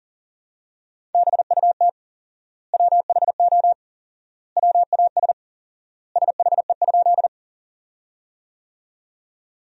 Morse code, heard as a keyed tone, about 40 wpm, 700 Hz. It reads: BUT WHO WAS SHE?
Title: 4x Word-Spacing - Sentences from Top 100 Words - 40wpm